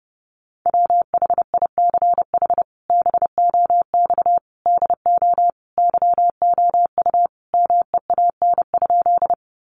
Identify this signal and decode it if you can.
Morse code, heard as a keyed tone, about 30 wpm, 700 Hz. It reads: WHICH BOX DO YOU MEAN?
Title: Sentences from Top 500 Words 30wpm